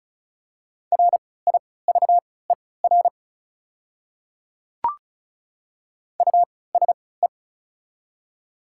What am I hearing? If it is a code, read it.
Morse code, heard as a keyed tone, about 36 wpm, spaced as for 20 wpm, 700 Hz. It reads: RIVER USE
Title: 3x Character Spacing - Top 300 Words - 35wpm